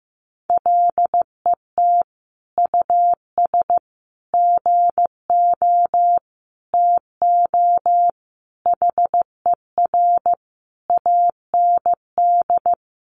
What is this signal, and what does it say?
Morse code, heard as a keyed tone, about 15 wpm, 700 Hz. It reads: LET US GO TO HER AND
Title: Sentences from Top 1100 Words 15wpm